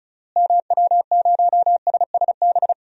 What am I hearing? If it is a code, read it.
Morse code, heard as a keyed tone, about 36 wpm, 700 Hz. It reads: MW0SSB
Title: DXCC-100 Call Signs - 6 Characters - Repeated 3x - 35wpm